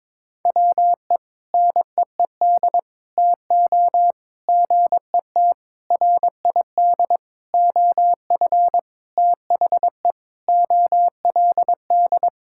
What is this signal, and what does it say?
Morse code, heard as a keyed tone, about 22 wpm, 700 Hz. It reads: WE NEED TO GET RID OF THE OLD
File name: practice / 22wpm / Sentences with 4-Word Phrases - 22wpm